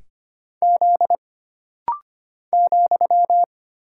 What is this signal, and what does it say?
Morse code, heard as a keyed tone, about 25 wpm, 700 Hz. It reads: Z ,
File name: Single Character (Letter:Number:Punctuation:Prosign) - Rapid-Fire - 25wpm